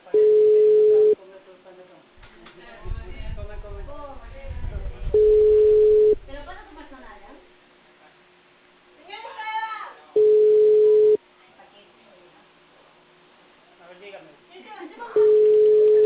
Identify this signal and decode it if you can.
Ringback tone (Europe, 425 Hz), with speech.